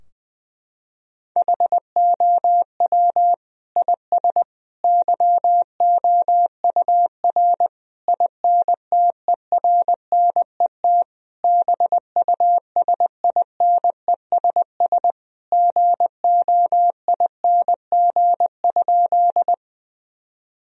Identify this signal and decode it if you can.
Morse code, heard as a keyed tone, about 21 wpm, 700 Hz. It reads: HOW IS YOUR INTERNET BUSINESS GOING?
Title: Sentences from Top 1000 Words - Review Part 4 - 20wpm